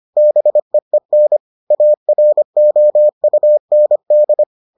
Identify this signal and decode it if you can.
Morse code, heard as a keyed tone, about 25 wpm, 600 Hz. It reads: BEEN AROUND